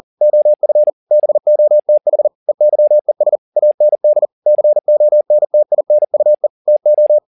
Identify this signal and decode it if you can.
Morse code, heard as a keyed tone, about 40 wpm, 600 Hz. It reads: OF BOTH EYES AND CONTINUE TO